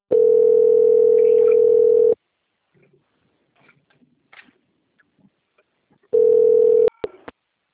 Ringback tone (North America), 440 + 480 Hz.
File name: out- - - - - .6394